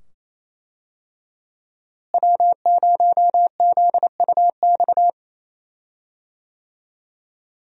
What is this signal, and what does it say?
Morse code, heard as a keyed tone, about 28 wpm, 700 Hz. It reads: W0ZUX
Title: USA Call Signs - 5 Characters - Repeated 2x - 28wpm